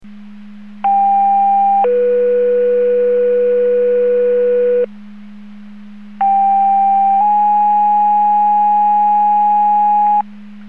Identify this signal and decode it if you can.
Two-tone page, heard as a one-second tone, then a three-second tone, about 790 then 490 Hz, about 790 then 830 Hz. It is heard from 0.8 to 10.2 s.